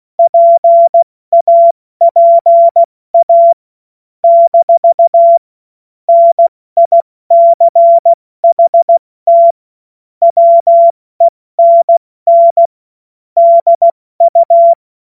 Morse code, heard as a keyed tone, about 16 wpm, 680 Hz. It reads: PAPA - NICHT WENN DU
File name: Morsekurs / Witze / Witz 03. August 2021, 70 F80